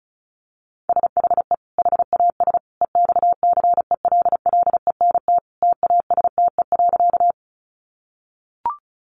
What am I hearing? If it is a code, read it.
Morse code, heard as a keyed tone, about 34 wpm, 700 Hz. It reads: SHE HAS EXCELLENT TASTE.
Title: Sentences from Top 1100 Words 35wpm